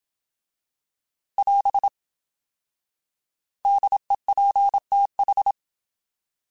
Morse code, heard as a keyed tone, about 27 wpm, 800 Hz. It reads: <AS> DEPTH